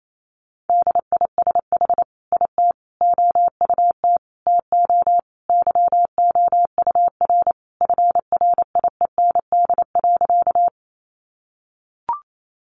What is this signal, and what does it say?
Morse code, heard as a keyed tone, about 28 wpm, 700 Hz. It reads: DISH IT OUT TO YOUR FRIEND.
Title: Sentences from Top 1300 Words 28wpm